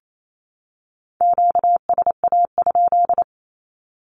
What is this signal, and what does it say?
Morse code, heard as a keyed tone, about 28 wpm, 700 Hz. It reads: QSA?
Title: Q-Codes 28wpm